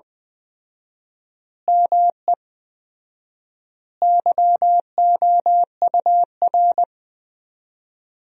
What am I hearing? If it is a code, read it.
Morse code, heard as a keyed tone, about 20 wpm, 700 Hz. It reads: ME YOUR
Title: 4x Word-Spacing - Sentences from Top 200 Words - 20wpm